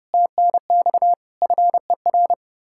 Morse code, heard as a keyed tone, about 36 wpm, 700 Hz. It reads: TNX FER
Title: Speed-Racing Common QSO Phrases Encore - 30wpm